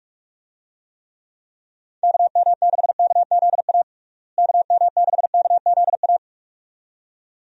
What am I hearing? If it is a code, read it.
Morse code, heard as a keyed tone, about 46 wpm, 700 Hz. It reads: KM6KZA KM6KZA
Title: USA Call Signs - 6 Characters - Repeated 2x - 45wpm